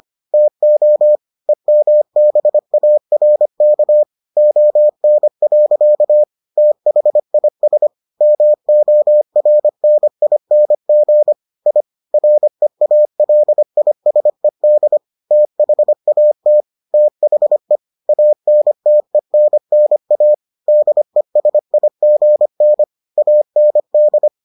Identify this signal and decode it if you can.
Morse code, heard as a keyed tone, about 25 wpm, 600 Hz. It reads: TO EMBARK ON. THIS MORNING I REALISED THAT THE ANTENNA DESIGN AND